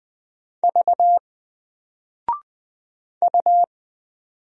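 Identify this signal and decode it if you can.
Morse code, heard as a keyed tone, about 20 wpm, 700 Hz. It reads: V U
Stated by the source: Single letter and number Morse drill at 20 wpm, from N4LCD.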